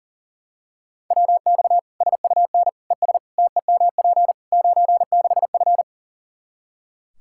Morse code, heard as a keyed tone, about 40 wpm, 700 Hz. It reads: WX SUN ES TEMP 96F
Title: Practice QSO - Exchange 2 - Info - 40wpm